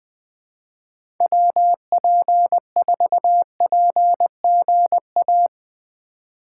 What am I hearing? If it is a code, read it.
Morse code, heard as a keyed tone, about 20 wpm, 700 Hz. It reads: WP4PGA
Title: USA Call Signs - 6 Characters - 4x Character Spacing - 20wpm